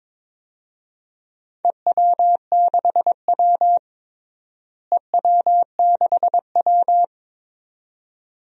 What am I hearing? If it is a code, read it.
Morse code, heard as a keyed tone, about 22 wpm, 700 Hz. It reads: EW6W EW6W
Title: DXCC-100 Call Signs - 4 Characters - Repeated 2x - 22wpm